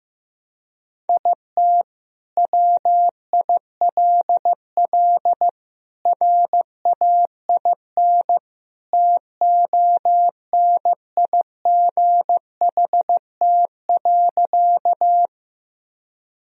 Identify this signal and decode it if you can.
Morse code, heard as a keyed tone, about 15 wpm, 700 Hz. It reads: IT WILL RAIN TONIGHT.
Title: Sentences from Top 1000 Words - Review Part 1 - 15wpm